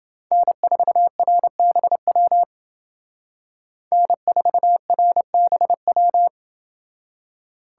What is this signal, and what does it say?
Morse code, heard as a keyed tone, about 28 wpm, 700 Hz. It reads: N4RBW N4RBW